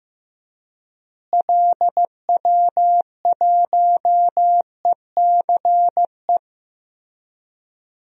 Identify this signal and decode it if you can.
Morse code, heard as a keyed tone, about 15 wpm, 700 Hz. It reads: LW1ECE
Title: DXCC-100 Call Signs - 6 Characters - Repeated 3x - 15wpm